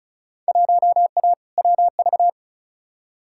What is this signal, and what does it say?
Morse code, heard as a keyed tone, about 36 wpm, 700 Hz. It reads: 1A WV
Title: Field Day Exchanges Encore - 35wpm